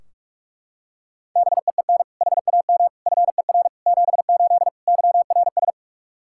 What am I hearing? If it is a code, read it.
Morse code, heard as a keyed tone, about 42 wpm, 700 Hz. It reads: BEEN HAM FER 78 YRS
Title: Practice QSO - Exchange 3 - More Info - 45wpm